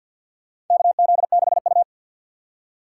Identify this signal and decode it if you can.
Morse code, heard as a keyed tone, about 53 wpm, 700 Hz. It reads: KZ6U
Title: USA Call Signs - 4 Characters - 6x Character Spacing - 50wpm